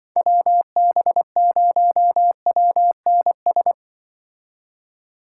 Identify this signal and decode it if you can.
Morse code, heard as a keyed tone, about 24 wpm, 700 Hz. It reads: WB0WNS
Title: Speed Racing - USA 2x3 Call Signs 20wpm